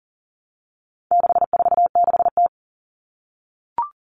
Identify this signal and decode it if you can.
Morse code, heard as a keyed tone, about 40 wpm, 700 Hz. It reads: 646T
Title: All Cut Numbers - 4-Digit 40wpm